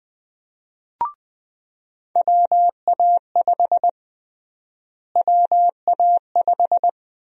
Morse code, heard as a keyed tone, about 20 wpm, 700 Hz. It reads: WA5 WA5
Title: USA Call Sign Prefixes - Repeated 2x - 20wpm